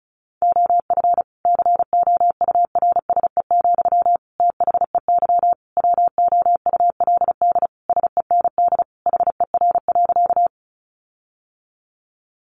Morse code, heard as a keyed tone, about 34 wpm, 700 Hz. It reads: OF COURSE, THEY WOULD SEND HER.
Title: Sentences from Top 1500 Words 35wpm